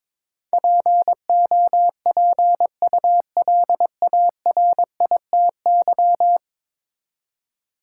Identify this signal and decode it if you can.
Morse code, heard as a keyed tone, about 22 wpm, 700 Hz. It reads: POPULARITY
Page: morse-code-ninja-practice-22wpm